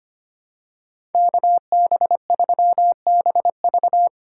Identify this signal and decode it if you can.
Morse code, heard as a keyed tone, about 25 wpm, 700 Hz. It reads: KB3BV
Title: SOTA QSO Practice 25wpm